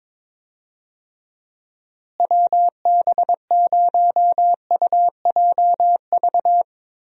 Morse code, heard as a keyed tone, about 22 wpm, 700 Hz. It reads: WB0UJV
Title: USA Call Signs - 6 Characters - Repeated 2x - 22wpm